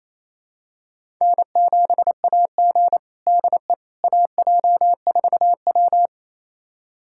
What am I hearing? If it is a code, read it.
Morse code, heard as a keyed tone, about 28 wpm, 700 Hz. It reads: N7AG DE AJ4W
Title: USA Callsign DE Callsign - 4 Characters - 28wpm